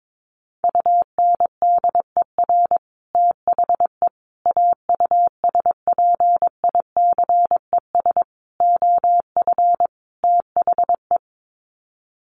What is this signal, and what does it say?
Morse code, heard as a keyed tone, about 22 wpm, 700 Hz. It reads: UNDER THE AUSPICES OF THE
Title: Sets of 5 Words Encore2 22wpm